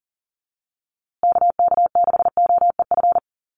Morse code, heard as a keyed tone, about 40 wpm, 700 Hz. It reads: KK6OEF
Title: USA Call Signs Repeated 3x 40wpm